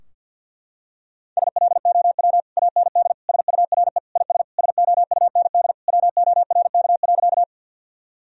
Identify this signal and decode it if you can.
Morse code, heard as a keyed tone, about 49 wpm, 700 Hz. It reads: SLOW AND SURE IS SOUND WORK.
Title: Sentences from Top 1000 Words - Review Part 2 - 50wpm